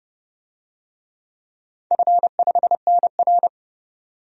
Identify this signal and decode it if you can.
Morse code, heard as a keyed tone, about 29 wpm, 700 Hz. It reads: F5NR
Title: DXCC-100 Call Signs - 4 Characters - 30wpm